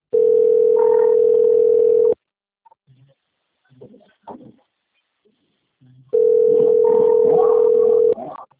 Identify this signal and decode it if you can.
Ringback tone (North America), 440 + 480 Hz.